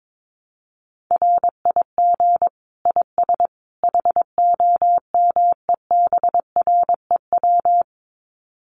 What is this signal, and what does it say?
Morse code, heard as a keyed tone, about 22 wpm, 700 Hz. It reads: RIG IS HOMEBREW
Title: Common QSO Phrases 22wpm